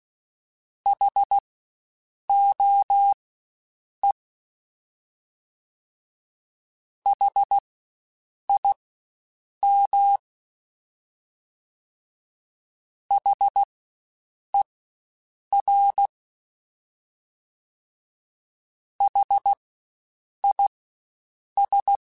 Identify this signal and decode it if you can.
Morse code, heard as a keyed tone, about 16 wpm, spaced as for 7 wpm, 800 Hz. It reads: HOE HIM HER HIS
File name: Spaced Alphas / Spaced Alphas mp3s / Hs sp @15